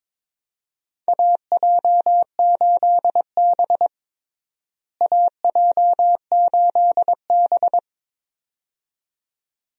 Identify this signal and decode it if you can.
Morse code, heard as a keyed tone, about 22 wpm, 700 Hz. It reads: AJ8B AJ8B
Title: USA Call Signs - 4 Characters - Repeated 2x - 22wpm